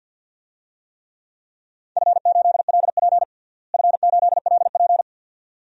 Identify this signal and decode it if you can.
Morse code, heard as a keyed tone, about 51 wpm, 700 Hz. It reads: F8LP F8LP
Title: DXCC-100 Call Signs - 4 Characters - Repeated 2x - 50wpm